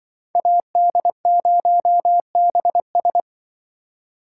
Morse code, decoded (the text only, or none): AD0BS